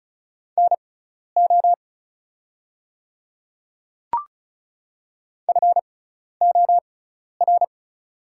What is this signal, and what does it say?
Morse code, heard as a keyed tone, about 36 wpm, spaced as for 12 wpm, 700 Hz. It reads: NO FOR
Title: 6x Character Spacing - Top 100 Words - 35wpm